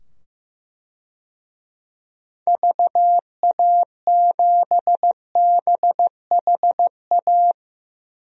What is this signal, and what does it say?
Morse code, heard as a keyed tone, about 15 wpm, 700 Hz. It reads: VA7BHA
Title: DXCC-100 Call Signs - 6 Characters - 15wpm